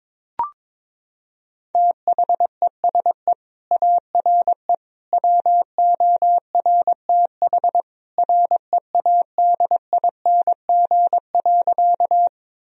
Morse code, heard as a keyed tone, about 22 wpm, 700 Hz. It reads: THESE ARE WORTH READING.
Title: Sentences from Top 1000 Words - Review Part 3 - 22wpm